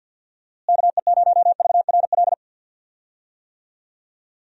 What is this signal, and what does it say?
Morse code, heard as a keyed tone, about 49 wpm, 700 Hz. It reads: KE0VRL
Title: USA Call Signs - 6 Characters - Repeated 3x - 50wpm